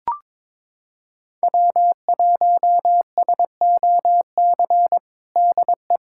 Morse code, heard as a keyed tone, about 21 wpm, 700 Hz. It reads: W1SOC DE
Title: USA Callsign DE Callsign - 5 Characters - 22wpm